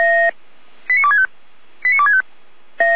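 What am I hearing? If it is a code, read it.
Five-tone selective call (ZVEI), heard as quick runs of five tones, tones of about 70 ms. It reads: 89265 89265
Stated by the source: ZVEI five-tone selective call received on SDR.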